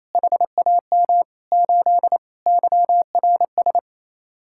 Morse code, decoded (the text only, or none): HAM 8 YRS